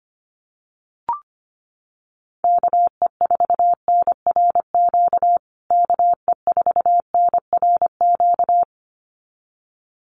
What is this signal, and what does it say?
Morse code, heard as a keyed tone, about 25 wpm, 700 Hz. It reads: KE4NRQ KE4NRQ
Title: USA Call Signs Repeated 2x 25wpm